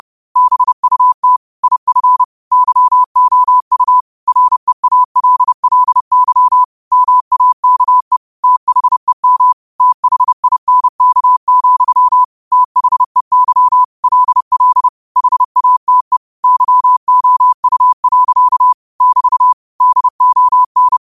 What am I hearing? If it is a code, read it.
Morse code, heard as a keyed tone, about 30 wpm, 1000 Hz. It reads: BUT IF YOU REALLY MAKE THEM THINK, THEY LL HATE YOU. <BT> DON
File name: qotdfiles / 30wpm